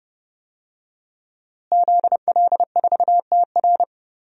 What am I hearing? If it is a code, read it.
Morse code, heard as a keyed tone, about 30 wpm, 700 Hz. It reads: ZL4TR
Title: DXCC-100 Call Signs - 5 Characters - Repeated 2x - 30wpm